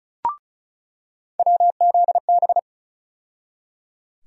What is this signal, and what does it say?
Morse code, heard as a keyed tone, about 36 wpm, 700 Hz. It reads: WZB
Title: USA Call Sign Postfixes - 3 Letters 35wpm